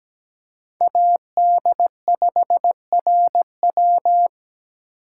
Morse code, heard as a keyed tone, about 17 wpm, 700 Hz. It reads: AD5RW
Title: USA Call Signs - 3x Character Spacing - 17wpm